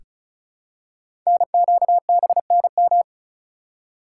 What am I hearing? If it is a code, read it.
Morse code, heard as a keyed tone, about 36 wpm, 700 Hz. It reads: NQBNM
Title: Code Groups - 5 Letters - 35wpm